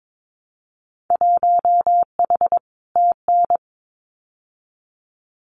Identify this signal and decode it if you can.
Morse code, heard as a keyed tone, about 22 wpm, 700 Hz. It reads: 1H TN
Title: Winter Field Day Exchanges 22wpm